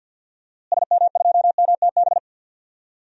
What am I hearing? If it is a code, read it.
Morse code, heard as a keyed tone, about 50 wpm, 700 Hz. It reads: SM2MTB
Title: DXCC-100 Call Signs - 6 Characters - Repeated 2x - 50wpm